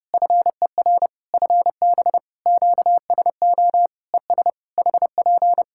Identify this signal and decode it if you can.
Morse code, heard as a keyed tone, about 30 wpm, 700 Hz. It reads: FER FB QSO ES HP
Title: Practice QSO - Exchange 4 - Ending - 30wpm